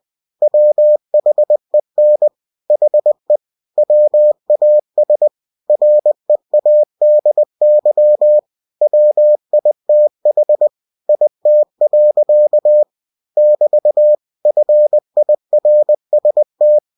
Morse code, heard as a keyed tone, about 20 wpm, 600 Hz. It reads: WHEN HE WAS READY WITH IT. <BT> FIRST
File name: winnie-20wpm